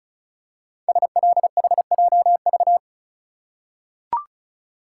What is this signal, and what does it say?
Morse code, heard as a keyed tone, about 36 wpm, 700 Hz. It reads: SLHJV
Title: Code Groups - 5 Letters - 35wpm